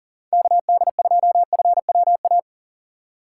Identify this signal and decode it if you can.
Morse code, heard as a keyed tone, about 40 wpm, 700 Hz. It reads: KD2FWA
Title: USA Call Signs - 6 Characters - 2x Character Spacing - 40wpm